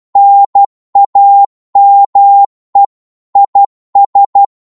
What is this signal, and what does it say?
Morse code, heard as a keyed tone, about 12 wpm, 800 Hz. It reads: NAME IS
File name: Morse / qso-12wpm